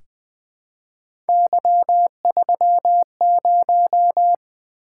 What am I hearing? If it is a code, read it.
Morse code, heard as a keyed tone, about 20 wpm, 700 Hz. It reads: Y30
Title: Three Letter-Number Combinations - 20wpm